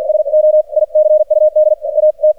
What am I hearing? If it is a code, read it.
Morse code, heard as a keyed tone, about 48 wpm, spaced as for 41 wpm, 610 Hz, with radio noise. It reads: SO ICUD AT
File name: archived / CRSnet